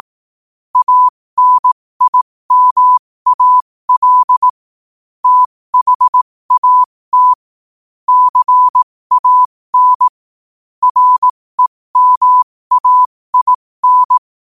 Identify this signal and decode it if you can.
Morse code, heard as a keyed tone, about 18 wpm, spaced as for 15 wpm, 1000 Hz. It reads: ANIMAL THAT CAN REMAIN
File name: qotdfiles / 15wpm